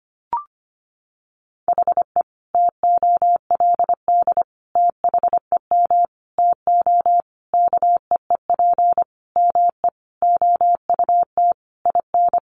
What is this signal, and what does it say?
Morse code, heard as a keyed tone, about 25 wpm, 700 Hz. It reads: HE TOLD THEM TO KEEP ME OUT IN
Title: Sentences from Top 1200 Words 25wpm